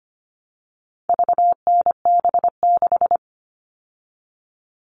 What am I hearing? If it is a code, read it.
Morse code, heard as a keyed tone, about 25 wpm, 700 Hz. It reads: VNB6
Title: All Cut Numbers - 4-Digit 25wpm